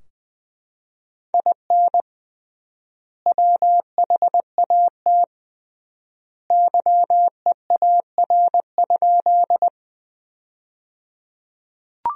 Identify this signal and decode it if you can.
Morse code, heard as a keyed tone, about 20 wpm, 700 Hz. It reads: IN WHAT YEAR?
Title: 3x Word-Spacing - Sentences from Top 200 Words - 20wpm